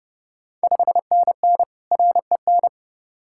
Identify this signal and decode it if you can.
Morse code, heard as a keyed tone, about 30 wpm, 700 Hz. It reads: 5NN REN
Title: Ontario QSO Party Exchange 30wpm